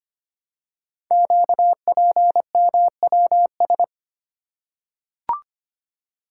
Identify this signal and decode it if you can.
Morse code, heard as a keyed tone, about 25 wpm, 700 Hz. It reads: QPMWS
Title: Code Groups - 5 Letters - 25wpm